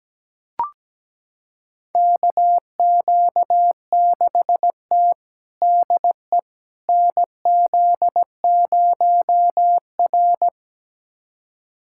Morse code, heard as a keyed tone, about 17 wpm, 700 Hz. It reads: KQ6T DE NZ0R
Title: USA Callsign DE Callsign - 4 Characters - 17wpm